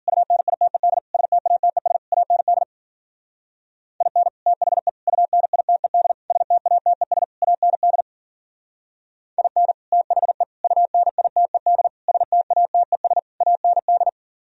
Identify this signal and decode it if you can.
Morse code, heard as a keyed tone, about 52 wpm, 700 Hz. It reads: UNITED STATES AND IN THE UNITED STATES AND IN THE UNITED STATES AND